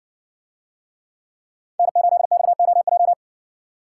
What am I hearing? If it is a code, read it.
Morse code, heard as a keyed tone, about 58 wpm, 700 Hz. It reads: N8XOJ